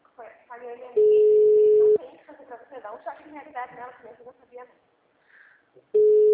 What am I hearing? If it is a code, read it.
Ringback tone (Europe, 425 Hz), with speech.
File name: out- -2010- - - .14